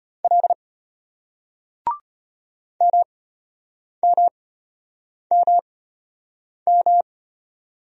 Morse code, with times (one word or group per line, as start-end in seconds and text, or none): 0.2-0.5 L
2.8-7.0 MMMM